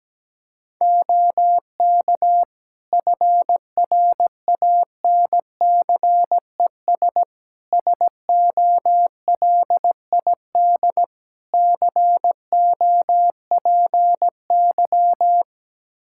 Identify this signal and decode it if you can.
Morse code, heard as a keyed tone, about 17 wpm, 700 Hz. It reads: OK FRANCES SOLID COPY
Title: Practice QSO - Exchange 3 - More Info - 17wpm